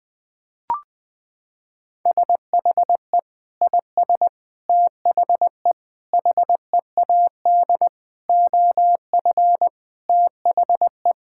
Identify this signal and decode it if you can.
Morse code, heard as a keyed tone, about 20 wpm, 700 Hz. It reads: SHE IS THE HEAD OF THE
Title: Sentences with 4-Word Phrases - 20wpm